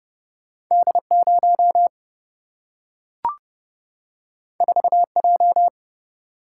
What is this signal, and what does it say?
Morse code, heard as a keyed tone, about 30 wpm, 700 Hz. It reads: D0 4J
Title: Single-Letter Single-Number 30wpm